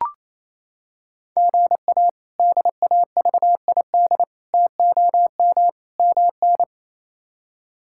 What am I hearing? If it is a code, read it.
Morse code, heard as a keyed tone, about 28 wpm, 700 Hz. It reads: GA DAVID TOM MN
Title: K1USN SST QSO Practice - 28wpm